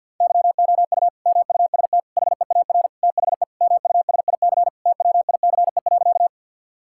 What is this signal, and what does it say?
Morse code, heard as a keyed tone, about 50 wpm, 700 Hz. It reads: YOU MUST HEAR THE MUSIC TWICE.
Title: Sentences from Top 1600 Words 50wpm